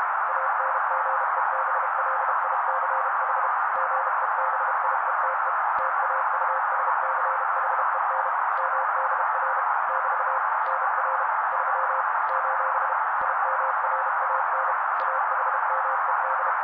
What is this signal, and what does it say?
Morse code, heard as a keyed tone, about 32 wpm, 530 Hz. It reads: ANGEBLICH GEBIETE TAUSCHEN MDR <BT> NR 3 : IMRAN AHMED
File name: daily / 31wpm noise